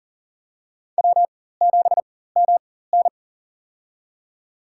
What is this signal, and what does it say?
Morse code, heard as a keyed tone, about 40 wpm, spaced as for 19 wpm, 700 Hz. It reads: W7MN
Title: USA Call Signs - 4 Characters - 4x Character Spacing - 40wpm